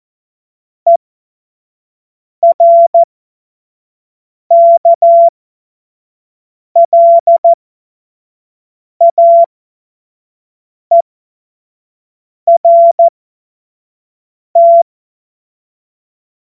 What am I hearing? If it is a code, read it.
Morse code, heard as a keyed tone, about 14 wpm, spaced as for 5 wpm, 680 Hz. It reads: ERKLAERT
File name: Morsekurs / Witze / Witz 09. Februar 2021, 25 F70